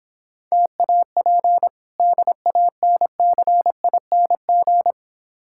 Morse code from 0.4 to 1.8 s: TAP

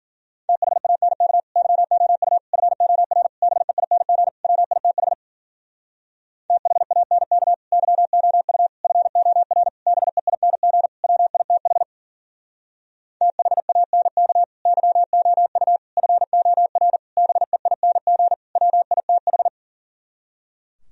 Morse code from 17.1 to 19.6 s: BEING WITH

